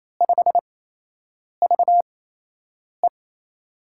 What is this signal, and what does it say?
Morse code, heard as a keyed tone, about 28 wpm, spaced as for 8 wpm, 700 Hz. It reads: 5VE
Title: USA Call Signs - 5 Characters - 8x Character Spacing - 28wpm